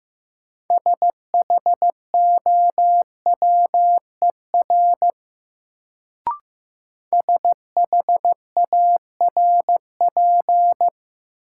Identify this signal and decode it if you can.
Morse code, heard as a keyed tone, about 15 wpm, 700 Hz. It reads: SHOWER SHARP
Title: Sentences from Top 1600 Words - New Word Drill Alt - 15wpm